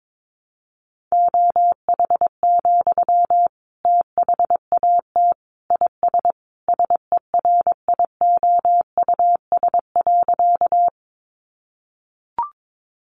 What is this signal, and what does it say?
Morse code, heard as a keyed tone, about 22 wpm, 700 Hz. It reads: OH, THAT IS SERIOUS.
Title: Sentences from Top 1100 Words 22wpm